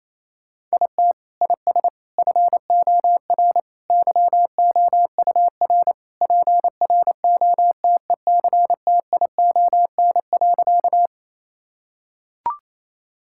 Morse code, heard as a keyed tone, about 28 wpm, 700 Hz. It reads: IT IS FOR YOUR PROTECTION.
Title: Sentences from Top 1600 Words 28wpm